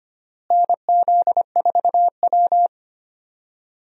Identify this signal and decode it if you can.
Morse code, heard as a keyed tone, about 25 wpm, 700 Hz. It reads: NZ4W
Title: USA Call Signs - 4 Characters - Repeated 3x - 25wpm